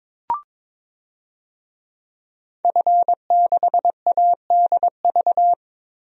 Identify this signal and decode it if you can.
Morse code, heard as a keyed tone, about 22 wpm, 700 Hz. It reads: F6ADV
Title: DXCC-100 Call Signs - 5 Characters - Repeated 3x - 22wpm